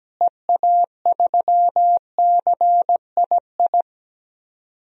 Morse code, heard as a keyed tone, about 17 wpm, 700 Hz. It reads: EA3CII
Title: DXCC-100 Call Signs - 6 Characters - Repeated 2x - 17wpm